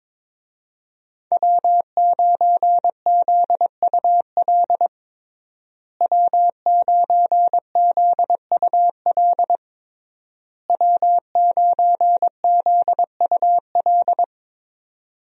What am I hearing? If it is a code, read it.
Morse code, heard as a keyed tone, about 22 wpm, 700 Hz. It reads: W9ZUL W9ZUL W9ZUL
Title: USA Call Signs - 5 Characters - Repeated 3x - 22wpm